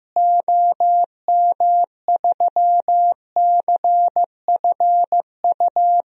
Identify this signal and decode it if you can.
Morse code, heard as a keyed tone, about 15 wpm, 700 Hz. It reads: OM3CFU
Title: DXCC-100 Call Signs - 6 Characters - Repeated 3x - 15wpm